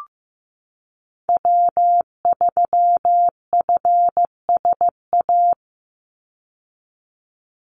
Morse code, heard as a keyed tone, about 15 wpm, 700 Hz. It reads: W3FSA
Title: Call Signs Repeated 2x - 15wpm